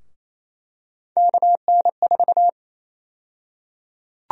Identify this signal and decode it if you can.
Morse code, heard as a keyed tone, about 28 wpm, 700 Hz. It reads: KN4
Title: USA Call Sign Prefixes - 28wpm